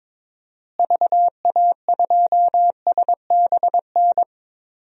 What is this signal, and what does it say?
Morse code, heard as a keyed tone, about 22 wpm, 700 Hz. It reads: VA2SBN
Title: DXCC-100 Call Signs - 6 Characters - Repeated 2x - 22wpm